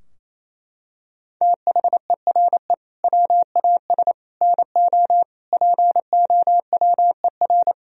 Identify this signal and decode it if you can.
Morse code, heard as a keyed tone, about 28 wpm, 700 Hz. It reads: THERE WAS NO POWER
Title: Sentences from Top 1000 Words - Review Part 3 - 28wpm